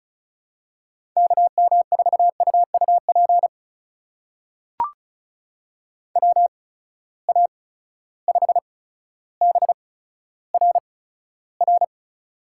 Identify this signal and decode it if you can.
Morse code, heard as a keyed tone, about 36 wpm, spaced as for 15 wpm, 700 Hz. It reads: KM4UUP WA5BRR
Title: USA Call Signs - 6 Characters - 8x Character Spacing - 35wpm